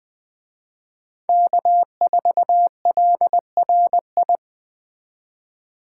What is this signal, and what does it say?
Morse code, heard as a keyed tone, about 20 wpm, 700 Hz. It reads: K4LRI
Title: Speed Racing - USA Call Signs 17wpm